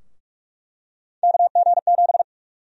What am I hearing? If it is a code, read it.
Morse code, heard as a keyed tone, about 42 wpm, 700 Hz. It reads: KG7
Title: USA Call Sign Prefixes - Repeated 2x - 45wpm